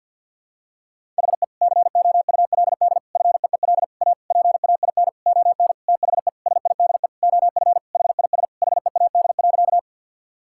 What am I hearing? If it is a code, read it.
Morse code, heard as a keyed tone, about 51 wpm, 700 Hz. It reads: HE COULD FEEL A PAIN ON THE SIDE OF HIS HEAD.
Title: Sentences from Top 1600 Words 50wpm